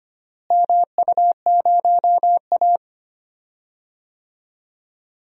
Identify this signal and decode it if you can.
Morse code, heard as a keyed tone, about 25 wpm, 700 Hz. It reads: MU0A
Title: DXCC-100 Call Signs - 4 Characters - Repeated 2x - 25wpm